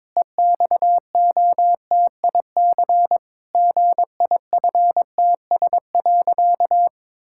Morse code, heard as a keyed tone, about 22 wpm, 700 Hz. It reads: EXOTIC GIFTS.
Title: Pangram Challenge v2 - 22wpm